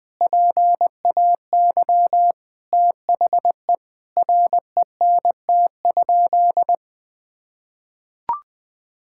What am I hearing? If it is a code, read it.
Morse code, heard as a keyed tone, about 20 wpm, 700 Hz. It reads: PAY THE RENT?
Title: Sentences from Top 1000 Words - Review Part 5 - 20wpm